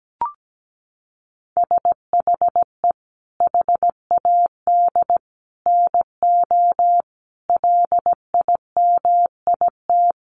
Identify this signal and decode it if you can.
Morse code, heard as a keyed tone, about 17 wpm, 700 Hz. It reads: SHE HAD NO LIMIT
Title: Sentences from Top 1500 Words 17wpm